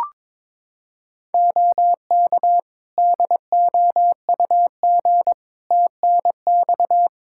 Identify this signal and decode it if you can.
Morse code, heard as a keyed tone, about 22 wpm, 700 Hz. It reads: OK DOUG TNX
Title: K1USN SST QSO Practice 22wpm